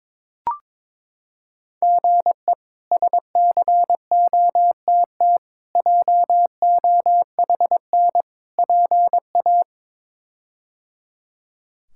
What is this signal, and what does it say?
Morse code, heard as a keyed tone, about 22 wpm, 700 Hz. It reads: GE SCOTT JOHN PA
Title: K1USN SST QSO Practice - 22wpm